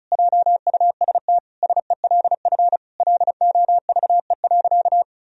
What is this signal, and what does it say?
Morse code, heard as a keyed tone, about 36 wpm, 700 Hz. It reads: JUST SELF LOVE.
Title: Sentences from Top 1000 Words - Review Part 1 - 35wpm